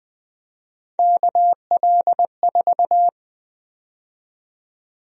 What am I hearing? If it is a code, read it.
Morse code, heard as a keyed tone, about 20 wpm, 700 Hz. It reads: KL4
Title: USA Call Sign Prefixes - Repeated 2x - 20wpm